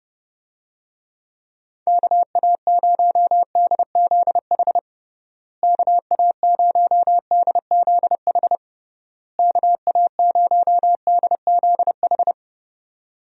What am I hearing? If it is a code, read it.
Morse code, heard as a keyed tone, about 30 wpm, 700 Hz. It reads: KA0DZH KA0DZH KA0DZH
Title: USA Call Signs - 6 Characters - Repeated 3x - 30wpm